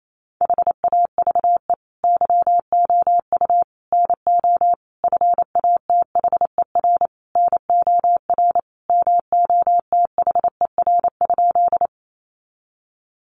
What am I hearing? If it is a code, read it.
Morse code, heard as a keyed tone, about 28 wpm, 700 Hz. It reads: HAVE YOU NO FATHER NOR MOTHER?
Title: Sentences from Top 1500 Words 28wpm